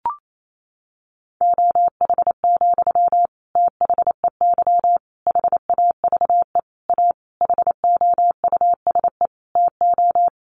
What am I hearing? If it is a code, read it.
Morse code, heard as a keyed tone, about 28 wpm, 700 Hz. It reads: OH, THEY HAVE A HOUSE TO
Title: Sentences from Top 800 Words 28wpm